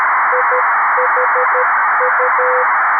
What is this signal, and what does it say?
Morse code, heard as a keyed tone, about 13 wpm, 500 Hz, with radio noise. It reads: IHU
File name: arissat / recordings / ARISSat 11Sept2011 1359utc 2mins 14frames